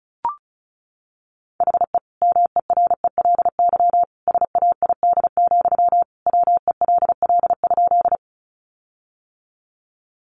Morse code, heard as a keyed tone, about 34 wpm, 700 Hz. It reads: HE MERELY SAID, WELL?
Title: Sentences from Top 1400 Words 35wpm